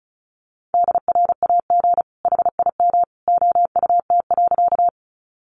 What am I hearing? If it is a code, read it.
Morse code, heard as a keyed tone, about 34 wpm, 700 Hz. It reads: DRAG HIM OUT.
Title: Sentences from Top 1500 Words 35wpm